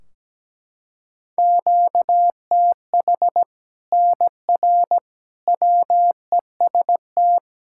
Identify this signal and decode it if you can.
Morse code, heard as a keyed tone, about 17 wpm, 700 Hz. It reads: QTH NR WEST
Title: Practice QSO - Exchange 1 - Beginning - 17wpm